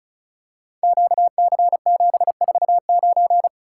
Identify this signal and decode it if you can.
Morse code, heard as a keyed tone, about 36 wpm, 700 Hz. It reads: QC749
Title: Code Groups - 5 Letter:Numbers - 35wpm